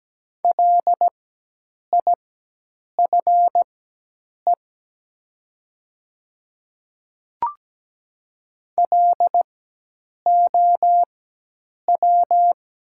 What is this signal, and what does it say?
Morse code, heard as a keyed tone, about 17 wpm, spaced as for 8 wpm, 700 Hz. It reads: LIFE LOW
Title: 4x Character Spacing - Top 300 Words - 17wpm